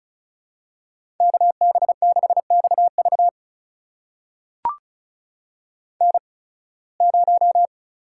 Morse code, heard as a keyed tone, about 36 wpm, spaced as for 23 wpm, 700 Hz. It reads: KB6XV N0
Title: USA Call Signs - 5 Characters - 8x Character Spacing - 35wpm